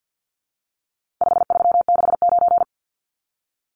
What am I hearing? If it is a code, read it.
Morse code, heard as a keyed tone, about 50 wpm, 700 Hz. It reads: 5369
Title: Numbers - 4-Digit 50wpm